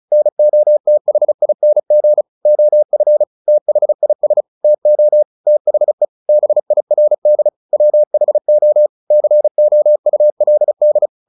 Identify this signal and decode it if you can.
Morse code, heard as a keyed tone, about 34 wpm, 600 Hz. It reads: NOTHING OF THIS TO THE BIRD WHO COULD
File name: the old man and the sea / the old man and the sea 1 35wpm